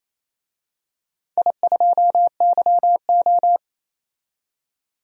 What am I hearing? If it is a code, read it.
Morse code, heard as a keyed tone, about 27 wpm, 700 Hz. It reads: I2YO
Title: DXCC-100 Call Signs - 4 Characters - 28wpm